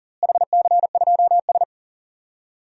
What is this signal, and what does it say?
Morse code, heard as a keyed tone, about 40 wpm, 700 Hz. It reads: HC2S
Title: DXCC-100 Call Signs - 4 Characters - Repeated 2x - 40wpm